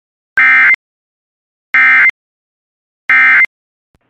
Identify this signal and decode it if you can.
Emergency Alert System (SAME) tones, heard as harsh buzzing bursts. It is heard from 0.3 to 3.4 s.